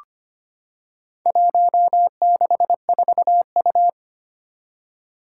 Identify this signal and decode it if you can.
Morse code, heard as a keyed tone, about 25 wpm, 700 Hz. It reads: 164U
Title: Four Challenging Letter-Number Combinations 25wpm